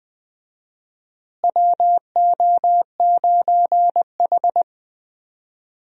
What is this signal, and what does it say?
Morse code, heard as a keyed tone, about 20 wpm, 700 Hz. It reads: WO9H
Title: USA Call Signs - 4 Characters - Repeated 3x - 20wpm